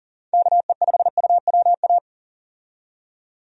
Morse code, heard as a keyed tone, about 40 wpm, 700 Hz. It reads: KE5UWA
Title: USA Call Signs - 6 Characters - 40wpm